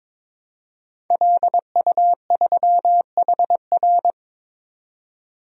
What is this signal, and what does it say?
Morse code, heard as a keyed tone, about 22 wpm, 700 Hz. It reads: LU3HR
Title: DXCC-100 Call Signs - 5 Characters - 22wpm